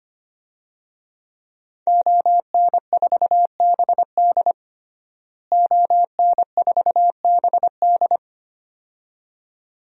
Morse code, heard as a keyed tone, about 25 wpm, 700 Hz. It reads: ON4BD ON4BD
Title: DXCC-100 Call Signs - 5 Characters - Repeated 2x - 25wpm